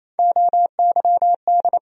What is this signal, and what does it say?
Morse code, heard as a keyed tone, about 28 wpm, 700 Hz. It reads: OYD
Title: USA Call Sign Postfixes - 3 Letters 28wpm